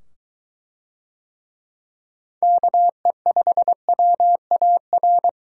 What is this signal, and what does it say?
Morse code, heard as a keyed tone, about 21 wpm, 700 Hz. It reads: KE5WAR